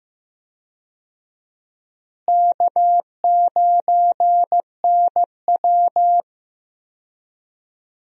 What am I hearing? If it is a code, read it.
Morse code, heard as a keyed tone, about 15 wpm, 700 Hz. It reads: K9NW
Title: USA Call Signs - 4 Characters - Repeated 2x - 15wpm